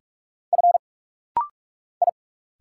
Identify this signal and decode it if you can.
Morse code, heard as a keyed tone, about 46 wpm, 700 Hz. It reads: F I
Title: Single Letter-Number Alt - Rapid-Fire 45wpm